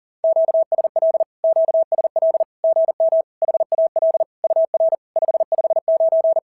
Morse code, heard as a keyed tone, about 40 wpm, 660 Hz. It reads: QSL QSL GM HAL UR 559
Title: SOTA QSO Practice 40wpm